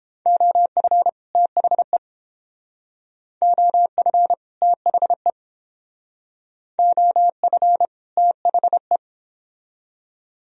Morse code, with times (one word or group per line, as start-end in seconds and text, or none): 0.3-1.1 OF
1.4-2.0 THE
3.4-4.3 OF
4.6-5.3 THE
6.8-7.9 OF
8.2-9.0 THE